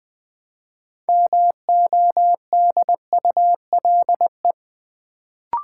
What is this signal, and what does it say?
Morse code, heard as a keyed tone, about 20 wpm, 700 Hz. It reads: MODULE
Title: Build A Word - 2 Letters At A Time - 6-Letter Words - 20wpm